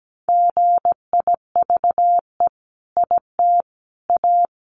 Morse code, heard as a keyed tone, about 17 wpm, 700 Hz. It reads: GIVE IT A
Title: Sentences from Top 1500 Words 17wpm